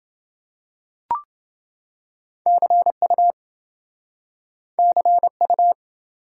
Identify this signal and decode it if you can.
Morse code, heard as a keyed tone, about 28 wpm, 700 Hz. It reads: CU CU